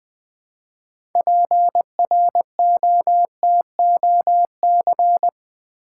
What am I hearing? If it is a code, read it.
Morse code, heard as a keyed tone, about 20 wpm, 700 Hz. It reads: PROTOC
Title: Build A Word - 3 Letters At A Time - 8-Letter Words - 20wpm